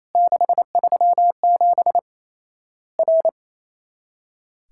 Morse code, heard as a keyed tone, about 28 wpm, 700 Hz. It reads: 637 R
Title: Worked All Europe DX Contest - QTC Exchange - 28wpm